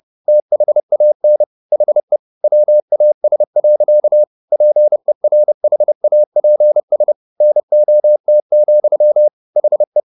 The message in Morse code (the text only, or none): THAN HE WAS. PERHAPS NOT, HE